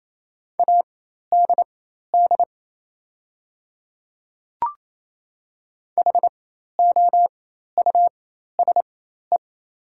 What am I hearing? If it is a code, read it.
Morse code, heard as a keyed tone, about 28 wpm, spaced as for 13 wpm, 700 Hz. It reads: ADD HOUSE
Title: 4x Character Spacing - Top 200 Words - 28wpm